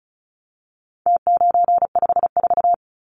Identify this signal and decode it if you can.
Morse code, heard as a keyed tone, about 34 wpm, 700 Hz. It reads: T954
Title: Numbers - Contest Style - 4-Digit 35wpm